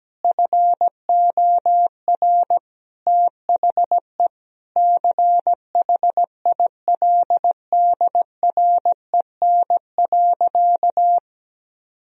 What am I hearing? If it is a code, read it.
Morse code, heard as a keyed tone, about 17 wpm, 700 Hz. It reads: FOR THE CHILDREN.
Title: Sentences from Top 1000 Words - Review Part 2 - 17wpm